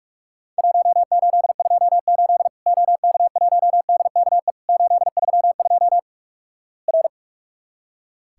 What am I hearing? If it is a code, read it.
Morse code, heard as a keyed tone, about 46 wpm, 700 Hz. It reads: 1828 OK1DKE 832 R
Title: Worked All Europe DX Contest - QTC Exchange - 45wpm